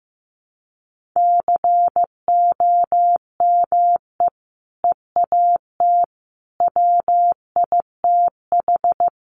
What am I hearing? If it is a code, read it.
Morse code, heard as a keyed tone, about 15 wpm, 700 Hz. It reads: COME EAT WITH
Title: Sentences from Top 300 Words 15wpm